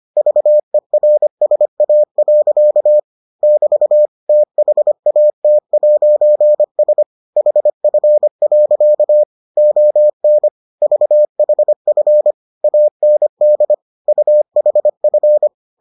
Morse code, heard as a keyed tone, about 25 wpm, 600 Hz. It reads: VERSA. <BT> THAT'S HF. ON VHF AND UHF